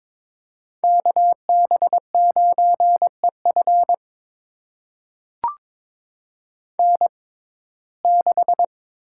Morse code, heard as a keyed tone, about 22 wpm, spaced as for 16 wpm, 700 Hz. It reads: KB9EF N6
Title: USA Call Signs - 5 Characters - 6x Character Spacing - 22wpm